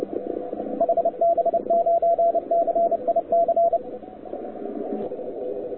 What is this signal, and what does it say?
Morse code, heard as a keyed tone, about 29 wpm, 680 Hz, with radio noise. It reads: HB9CIC
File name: wwcw08 / k5zd wwcw08 1300z